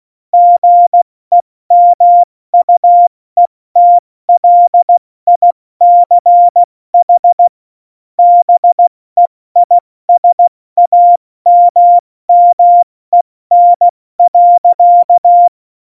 Morse code, heard as a keyed tone, about 16 wpm, spaced as for 14 wpm, 700 Hz. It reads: GEMUETLICH BEISAMMEN.
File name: Morsekurs / Witze / Witz 10.Dezember 2019, 70 F80